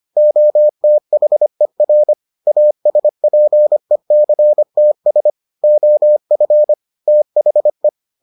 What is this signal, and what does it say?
Morse code, heard as a keyed tone, about 25 wpm, 600 Hz. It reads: OTHER ASPECTS OF THE